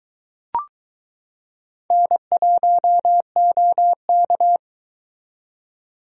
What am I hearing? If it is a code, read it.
Morse code, heard as a keyed tone, about 22 wpm, 700 Hz. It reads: N1OK